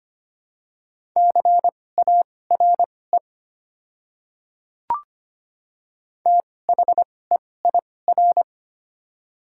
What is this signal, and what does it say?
Morse code, heard as a keyed tone, about 25 wpm, spaced as for 18 wpm, 700 Hz. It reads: CARE THEIR
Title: 2x Character Spacing - Top 500 Words - 25wpm